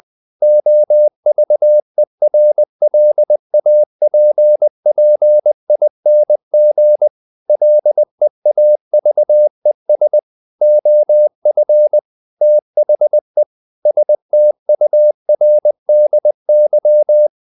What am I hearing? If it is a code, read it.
Morse code, heard as a keyed tone, about 20 wpm, 600 Hz. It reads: OVERLAPPING LEAVES OF THE STURDY
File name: the old man and the sea / the old man and the sea part 1 20wpm